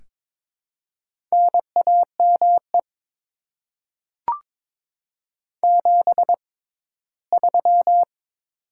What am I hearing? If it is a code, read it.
Morse code, heard as a keyed tone, about 23 wpm, spaced as for 15 wpm, 690 Hz. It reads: NAME 73
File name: Top 20 QSO Elements - 6x Character Spacing - 22wpm